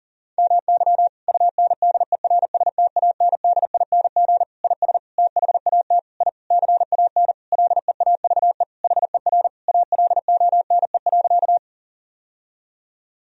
Morse code, heard as a keyed tone, about 40 wpm, 700 Hz. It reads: MY UNDERSTANDING IS THAT I CAN LEAVE HER ALONE.
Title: Sentences from Top 1000 Words - Review Part 3 - 40wpm